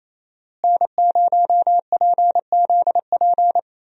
Morse code, heard as a keyed tone, about 28 wpm, 700 Hz. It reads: N0PZP
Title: USA Call Signs - 5 Characters - Repeated 3x - 28wpm